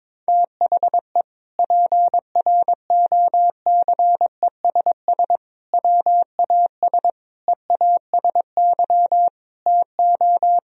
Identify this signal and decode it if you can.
Morse code, heard as a keyed tone, about 22 wpm, 700 Hz. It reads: THE PROCESS WAS EASY TO
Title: Sentences from Top 1000 Words - Review Part 3 - 22wpm